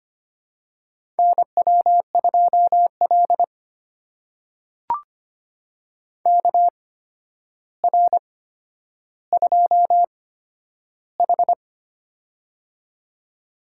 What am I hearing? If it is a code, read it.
Morse code, heard as a keyed tone, about 25 wpm, spaced as for 11 wpm, 700 Hz. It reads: NW2L KR2H
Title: USA Call Signs - 4 Characters - 8x Character Spacing - 25wpm